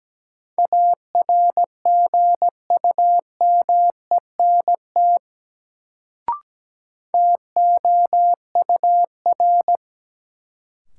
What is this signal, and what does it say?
Morse code, heard as a keyed tone, about 17 wpm, 700 Hz. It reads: ARGUMENT TOUR
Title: Sentences from Top 1400 Words - New Word Drill - 17wpm